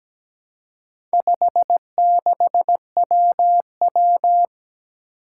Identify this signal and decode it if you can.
Morse code, heard as a keyed tone, about 17 wpm, 700 Hz. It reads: 56WW
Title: Four Challenging Letter-Number Combinations 17wpm